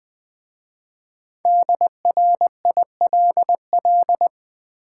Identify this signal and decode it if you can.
Morse code, heard as a keyed tone, about 20 wpm, 690 Hz. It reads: DRILL
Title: Easy Words with Hints - 20wpm